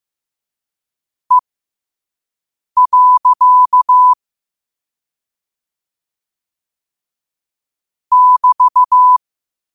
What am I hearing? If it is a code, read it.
Morse code, heard as a keyed tone, about 15 wpm, spaced as for 5 wpm, 1000 Hz. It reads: E. <BT>
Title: 5 WPM morse code quotes for Fri, 01 Mar 2024